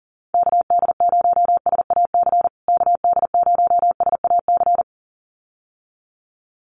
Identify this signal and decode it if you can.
Morse code, heard as a keyed tone, about 40 wpm, 700 Hz. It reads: KD0SAC KD0SAC
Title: USA Call Signs Repeated 2x 40wpm